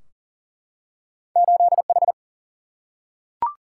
Morse code, heard as a keyed tone, about 40 wpm, 700 Hz. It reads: 8H
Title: Single-Letter Single-Number 40wpm